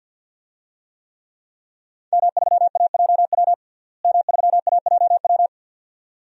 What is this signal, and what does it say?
Morse code, heard as a keyed tone, about 52 wpm, 700 Hz. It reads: M3AJW M3AJW
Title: DXCC-100 Call Signs - 5 Characters - Repeated 2x - 50wpm